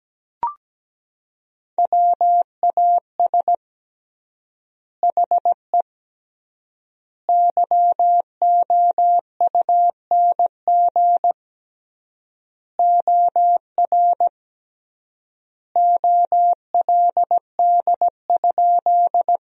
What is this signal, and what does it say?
Morse code, heard as a keyed tone, about 17 wpm, 700 Hz. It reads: WAS HE YOUNG OR OLD?
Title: 3x Word-Spacing - Sentences from Top 400 Words - 17wpm